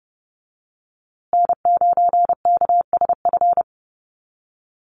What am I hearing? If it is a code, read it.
Morse code, heard as a keyed tone, about 30 wpm, 700 Hz. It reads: N9KSF
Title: USA Call Signs Repeated 2x 30wpm